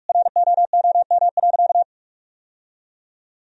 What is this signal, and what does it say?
Morse code, heard as a keyed tone, about 46 wpm, 700 Hz. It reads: ROOM.
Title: Sentences from Top 1000 Words - Review Part 3 - 45wpm